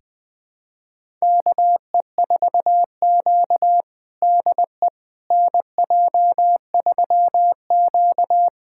Morse code, heard as a keyed tone, about 20 wpm, 700 Hz. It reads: KE4Q DE NJ3Q
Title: USA Callsign DE Callsign - 4 Characters - 20wpm